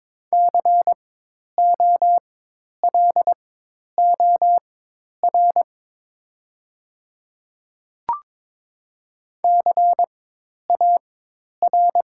Morse code, heard as a keyed tone, about 22 wpm, spaced as for 10 wpm, 690 Hz. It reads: COLOR CAR
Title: 4x Character Spacing - Top 500 Words - 22wpm